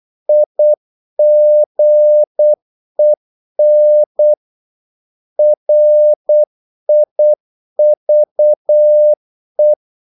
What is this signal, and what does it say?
Morse code, heard as a keyed tone, about 8 wpm, 600 Hz. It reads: IGEN RIVE
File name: Korte ord DK 8wpm